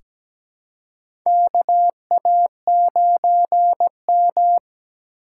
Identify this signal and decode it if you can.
Morse code, heard as a keyed tone, about 17 wpm, 700 Hz. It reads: KA9M
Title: USA Call Signs - 2x Character Spacing - 17wpm